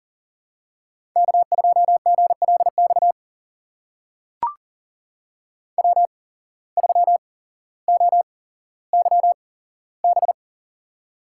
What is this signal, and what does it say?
Morse code, heard as a keyed tone, about 38 wpm, spaced as for 17 wpm, 700 Hz. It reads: K2GLX W3OYB
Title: USA Call Signs - 5 Characters - 8x Character Spacing - 40wpm